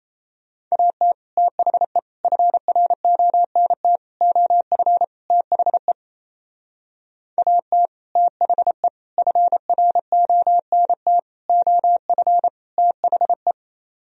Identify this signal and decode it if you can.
Morse code, heard as a keyed tone, about 34 wpm, 700 Hz. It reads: AT THE FRONT OF THE AT THE FRONT OF THE
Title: Speed-Racing - Sets of 5 Words 28wpm